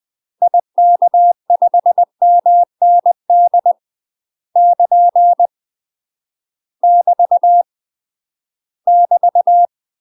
Morse code, heard as a keyed tone, about 20 wpm, 700 Hz. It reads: IK5MND <KN> <BT> <BT>
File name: UK MASTER QSOs TRAINING / 20WPM / UK MASTER QSOs 0023